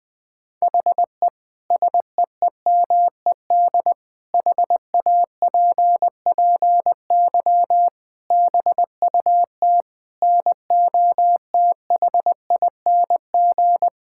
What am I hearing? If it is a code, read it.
Morse code, heard as a keyed tone, about 20 wpm, 700 Hz. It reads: HE SEEMED HAPPY BUT NOTHING
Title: Sentences with 8-Word Phrases - 20wpm